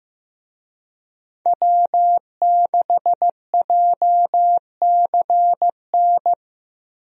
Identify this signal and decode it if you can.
Morse code, heard as a keyed tone, about 15 wpm, 700 Hz. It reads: W6JCN
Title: USA Call Signs - 5 Characters - Repeated 2x - 15wpm